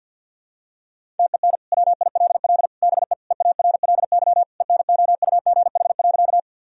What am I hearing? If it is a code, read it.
Morse code, heard as a keyed tone, about 50 wpm, 700 Hz. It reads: TEN WILL BE EARLY ENOUGH.
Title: Sentences from Top 1000 Words - Review Part 2 - 50wpm